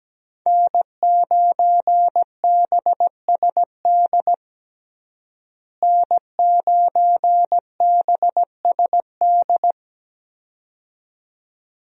Morse code, heard as a keyed tone, about 17 wpm, 700 Hz. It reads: N9BSD N9BSD
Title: USA Call Signs - 5 Characters - Repeated 2x - 17wpm